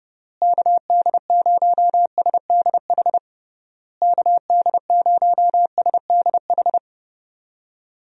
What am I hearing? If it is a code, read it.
Morse code, heard as a keyed tone, about 32 wpm, 700 Hz. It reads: KD0SDH KD0SDH
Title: USA Call Signs - 6 Characters - Repeated 3x - 30wpm